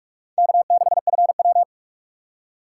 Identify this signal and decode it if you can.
Morse code, heard as a keyed tone, about 44 wpm, 700 Hz. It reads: K6FW